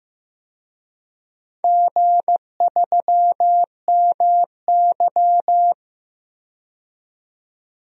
Morse code, heard as a keyed tone, about 16 wpm, 700 Hz. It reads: G3MY
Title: DXCC-100 Call Signs - 4 Characters - 15wpm